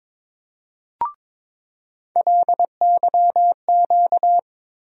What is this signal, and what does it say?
Morse code, heard as a keyed tone, about 22 wpm, 700 Hz. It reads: LYQ
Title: USA Call Sign Postfixes - 3 Letters 22wpm